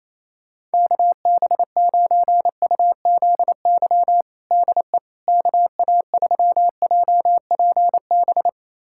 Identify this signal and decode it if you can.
Morse code, heard as a keyed tone, about 28 wpm, 700 Hz. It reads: KB9UZY DE KA3JPB
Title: USA Callsign DE Callsign - 6 Characters - 28wpm